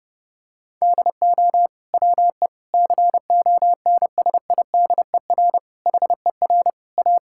Morse code, heard as a keyed tone, about 30 wpm, 700 Hz. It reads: DO WE CONSIDER HER A
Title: Sentences from Top 1000 Words - Review Part 5 - 30wpm